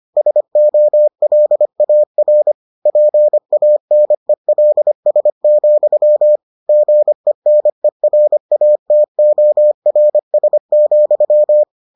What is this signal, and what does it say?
Morse code, heard as a keyed tone, about 25 wpm, 600 Hz. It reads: SOLAR PANELS, GENERATORS,